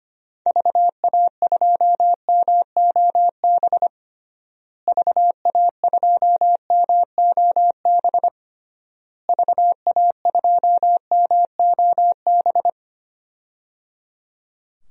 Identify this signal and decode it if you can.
Morse code, heard as a keyed tone, about 25 wpm, 700 Hz. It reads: VA2MOB VA2MOB VA2MOB
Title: DXCC-100 Call Signs - 6 Characters - Repeated 3x - 25wpm